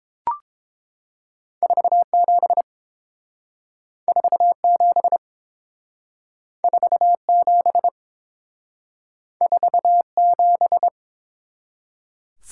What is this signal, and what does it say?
Morse code, heard as a keyed tone, about 29 wpm, 700 Hz. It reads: 47 47 47 47